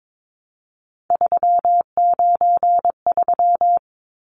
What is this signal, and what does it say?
Morse code, heard as a keyed tone, about 22 wpm, 700 Hz. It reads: 393
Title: Numbers - 3-Digit 22wpm